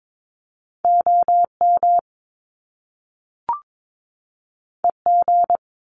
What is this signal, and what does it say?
Morse code, heard as a keyed tone, about 22 wpm, 700 Hz. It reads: OM EG
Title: Top Two Letter Combinations in Words 22wpm